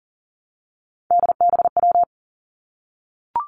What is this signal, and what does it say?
Morse code, heard as a keyed tone, about 40 wpm, 700 Hz. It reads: DBW
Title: Three Challenging Letter-Number Combinations 40wpm